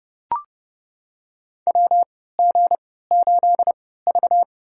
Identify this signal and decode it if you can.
Morse code, heard as a keyed tone, about 30 wpm, spaced as for 17 wpm, 700 Hz. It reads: WG8V
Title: USA Call Signs - 4 Characters - 3x Character Spacing - 30wpm